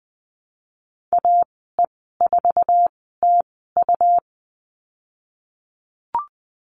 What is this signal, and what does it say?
Morse code, heard as a keyed tone, about 20 wpm, spaced as for 15 wpm, 700 Hz. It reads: AE4TU
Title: 122 Lesson 20g - Number 3 - Call Signs 20wpm